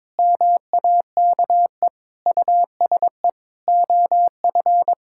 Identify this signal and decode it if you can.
Morse code, heard as a keyed tone, about 22 wpm, 700 Hz. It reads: MAKE USE OF
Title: Sentences from Top 1000 Words - Review Part 6 - 22wpm